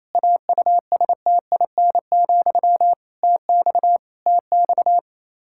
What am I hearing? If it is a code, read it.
Morse code, heard as a keyed tone, about 28 wpm, 700 Hz. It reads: AUSTIN, TX TX
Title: K1USN SST QSO Practice 28wpm